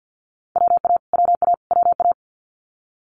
Morse code, heard as a keyed tone, about 50 wpm, 700 Hz. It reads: WA WA WA
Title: Multi-letter Phonemes Repeated 3x 50wpm